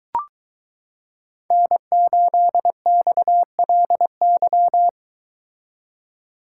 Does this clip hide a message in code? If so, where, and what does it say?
Yes: Morse code from 1.5 to 4.9 s, heard as a keyed tone, about 22 wpm, 700 Hz. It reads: N8XLY